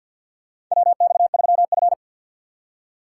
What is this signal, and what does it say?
Morse code, heard as a keyed tone, about 49 wpm, 690 Hz. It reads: WX3F
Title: USA Call Signs - 2x Character Spacing - 50wpm